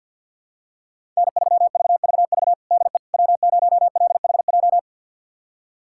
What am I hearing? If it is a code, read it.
Morse code, heard as a keyed tone, about 52 wpm, 700 Hz. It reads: N3VVV DE W0LHJ
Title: USA Callsign DE Callsign - 5 Characters - 50wpm